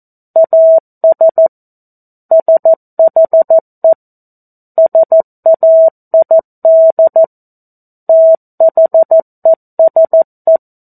Morse code, heard as a keyed tone, about 14 wpm, 650 Hz. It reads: AS SHE SAID THESE
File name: 02-Chapter-02 14 WPM